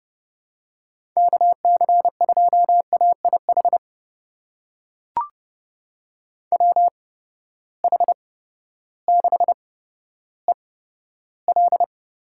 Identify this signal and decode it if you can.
Morse code, heard as a keyed tone, about 30 wpm, spaced as for 14 wpm, 700 Hz. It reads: KC2AIH WH6EL
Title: USA Call Signs - 6 Characters - 8x Character Spacing - 30wpm